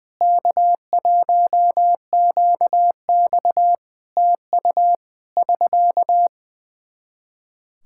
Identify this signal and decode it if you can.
Morse code, heard as a keyed tone, about 21 wpm, 700 Hz. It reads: K1QX TU <SK>
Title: Practice QSO - Exchange 4 - Ending - 20wpm